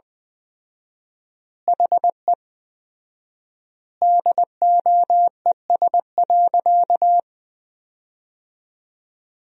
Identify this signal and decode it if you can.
Morse code, heard as a keyed tone, about 20 wpm, 700 Hz. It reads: HE DOES.
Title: 4x Word-Spacing - Sentences from Top 500 Words - 20wpm